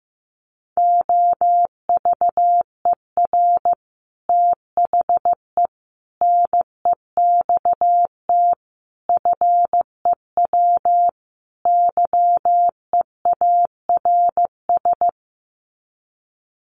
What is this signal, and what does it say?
Morse code, heard as a keyed tone, about 15 wpm, 700 Hz. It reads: OVER THE NEXT FEW YEARS
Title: Sets of 5 Words Encore 15wpm